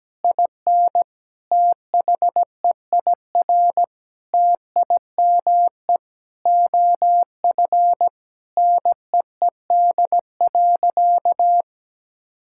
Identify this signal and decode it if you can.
Morse code, heard as a keyed tone, about 17 wpm, 700 Hz. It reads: IN THEIR TIME OF NEED.
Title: Sentences from Top 1000 Words - Review Part 5 - 17wpm